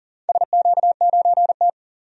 Morse code, heard as a keyed tone, about 39 wpm, 700 Hz. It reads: SQ9T
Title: DXCC-100 Call Signs - 4 Characters - 40wpm